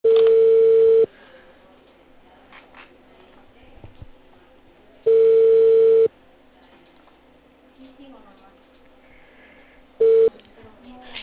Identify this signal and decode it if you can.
Ringback tone (China, 450 Hz), with speech.